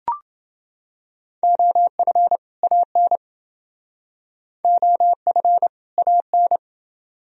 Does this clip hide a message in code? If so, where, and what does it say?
Yes: Morse code from 1.4 to 6.6 s, heard as a keyed tone, about 28 wpm, 700 Hz. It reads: OF AN OF AN